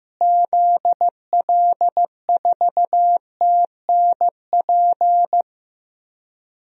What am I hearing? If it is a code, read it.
Morse code, heard as a keyed tone, about 15 wpm, 700 Hz. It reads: ZL4TNP
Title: DXCC-100 Call Signs - 6 Characters - Repeated 3x - 15wpm